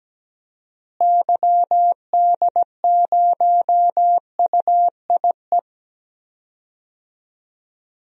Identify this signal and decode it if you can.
Morse code, heard as a keyed tone, about 17 wpm, 700 Hz. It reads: YD0UIE
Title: DXCC-100 Call Signs - 6 Characters - Repeated 2x - 17wpm